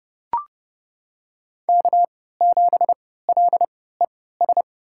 Morse code, heard as a keyed tone, about 30 wpm, spaced as for 17 wpm, 700 Hz. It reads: K7LES
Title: USA Call Signs - 5 Characters - 3x Character Spacing - 30wpm